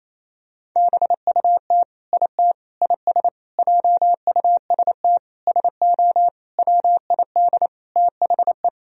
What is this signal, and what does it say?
Morse code, heard as a keyed tone, about 28 wpm, 700 Hz. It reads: BUT IT IS JUST SO WID THE
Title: Sentences from Top 200 Words and Abbreviations 28wpm